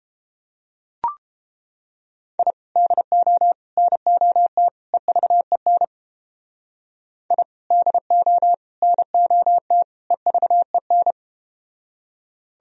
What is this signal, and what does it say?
Morse code, heard as a keyed tone, about 32 wpm, 700 Hz. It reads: I DO NOT EVEN I DO NOT EVEN